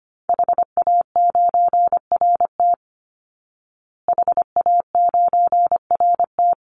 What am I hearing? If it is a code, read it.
Morse code, heard as a keyed tone, about 25 wpm, 700 Hz. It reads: HA9RT HA9RT
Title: Call Signs Repeated 3x - 25wpm